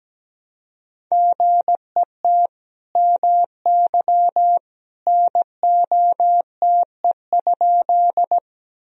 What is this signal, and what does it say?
Morse code, heard as a keyed tone, about 17 wpm, 700 Hz. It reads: GET MY NOTE?
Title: Sentences from Top 1000 Words - Review Part 2 - 17wpm